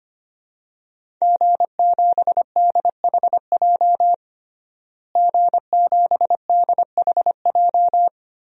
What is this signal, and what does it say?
Morse code, heard as a keyed tone, about 25 wpm, 700 Hz. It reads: G7DHJ G7DHJ
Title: DXCC-100 Call Signs - 5 Characters - Repeated 3x - 25wpm